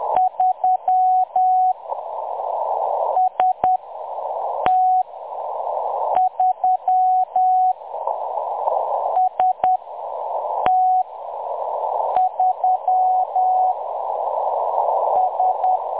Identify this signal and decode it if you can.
Morse code, heard as a keyed tone, about 10 wpm, spaced as for 6 wpm, 740 Hz. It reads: 3 ST3 ST3 S
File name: M23-ST3- -0858utc-10184khz